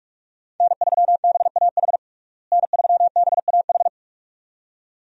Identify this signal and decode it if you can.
Morse code, heard as a keyed tone, about 47 wpm, 700 Hz. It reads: N3BAH N3BAH
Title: USA Call Signs - 5 Characters - Repeated 2x - 45wpm